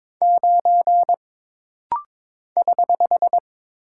Morse code, heard as a keyed tone, about 22 wpm, 700 Hz. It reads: 9 <HH>
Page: This rapid-fire timing is geared to push your ICR (Instant Character Recognition) skills.